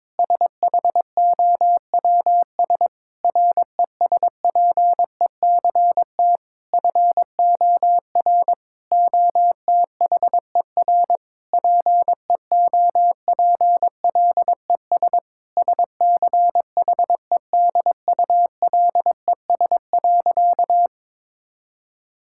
Morse code, heard as a keyed tone, about 23 wpm, 700 Hz. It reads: SHOWS RESPECT FOR OTHER PEOPLES SCHEDULES.
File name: Difficult Sentences with Hints - 22wpm